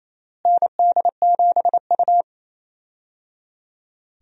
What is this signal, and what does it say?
Morse code, heard as a keyed tone, about 27 wpm, 700 Hz. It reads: ND7U
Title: USA Call Signs - 4 Characters - Repeated 3x - 28wpm